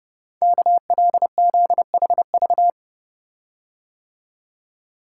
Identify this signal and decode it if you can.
Morse code, heard as a keyed tone, about 30 wpm, 700 Hz. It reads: KLZHV
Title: Code Groups - 5 Letters - 30wpm